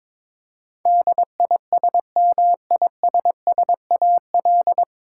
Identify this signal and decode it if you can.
Morse code, heard as a keyed tone, about 22 wpm, 700 Hz. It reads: DISMISSAL
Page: Morse Code Ninja practice 22wpm